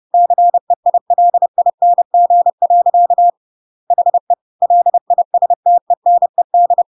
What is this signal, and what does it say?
Morse code, heard as a keyed tone, about 30 wpm, 700 Hz. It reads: CEILING. HE LISTENED
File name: Pinocchio / Pinocchio ch1 30 30 ik1hge